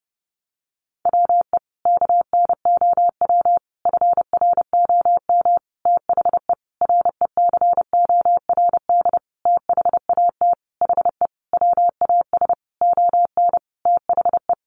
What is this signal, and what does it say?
Morse code, heard as a keyed tone, about 30 wpm, 700 Hz. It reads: WE KNOW FROM THE RECORD THAT HE WAS ON THE
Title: Sentences from Top 500 Words 30wpm